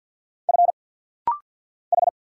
Morse code, heard as a keyed tone, about 50 wpm, 700 Hz. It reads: F H
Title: Single Letters - Rapid-Fire 50wpm